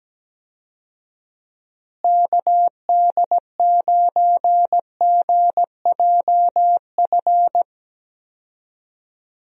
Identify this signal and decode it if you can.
Morse code, heard as a keyed tone, about 17 wpm, 700 Hz. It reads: KD9GJF
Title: USA Call Signs - 6 Characters - Repeated 2x - 17wpm